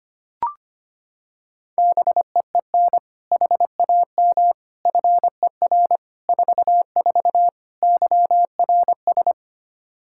Morse code, heard as a keyed tone, about 25 wpm, 700 Hz. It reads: BEEN HAM FER 44 YRS
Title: Practice QSO - Exchange 3 - More Info - 25wpm